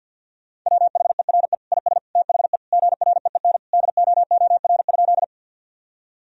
Morse code, heard as a keyed tone, about 52 wpm, 700 Hz. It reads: WHERE IS THE GREEN DOOR?
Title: Sentences from Top 1000 Words - Review Part 2 - 50wpm